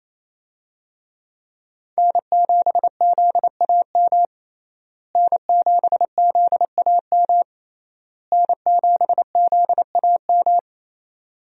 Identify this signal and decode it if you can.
Morse code, heard as a keyed tone, about 28 wpm, 700 Hz. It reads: N7ZAM N7ZAM N7ZAM
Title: USA Call Signs - 5 Characters - Repeated 3x - 28wpm